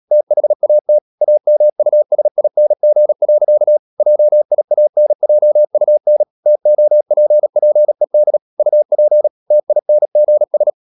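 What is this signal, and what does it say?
Morse code, heard as a keyed tone, about 37 wpm, 600 Hz. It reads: THAT AMUSING. JIANJUN TOPPED UP TINGS
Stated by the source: Morse code practice at 35–39 wpm sending English text of Ken Follett's novel Never, from CW med Gnister.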